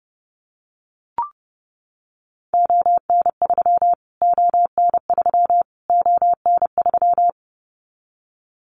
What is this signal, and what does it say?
Morse code, heard as a keyed tone, about 30 wpm, 700 Hz. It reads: ON3 ON3 ON3
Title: Callsign Prefixes for Top 25 DXCC - Repeated 3x - 30wpm